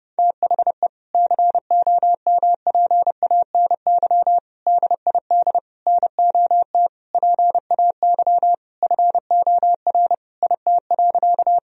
Morse code, heard as a keyed tone, about 30 wpm, 700 Hz. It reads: THE COMPANY DID NOT PAY FOR IT.
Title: Sentences from Top 1000 Words - Review Part 3 - 30wpm